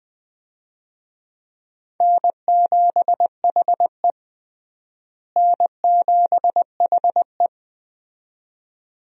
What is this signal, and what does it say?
Morse code, heard as a keyed tone, about 20 wpm, 700 Hz. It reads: N7HE N7HE
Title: USA Call Signs - 4 Characters - Repeated 2x - 20wpm